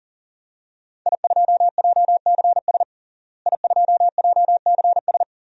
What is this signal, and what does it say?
Morse code, heard as a keyed tone, about 40 wpm, 700 Hz. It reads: I2JCS I2JCS
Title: DXCC-100 Call Signs - 5 Characters - Repeated 2x - 40wpm